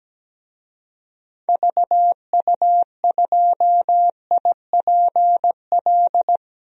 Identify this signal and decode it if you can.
Morse code, heard as a keyed tone, about 17 wpm, 700 Hz. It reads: VU2IPL
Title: DXCC-100 Call Signs - 6 Characters - Repeated 3x - 17wpm